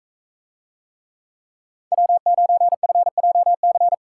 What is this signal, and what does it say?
Morse code, heard as a keyed tone, about 44 wpm, 700 Hz. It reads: W9FJC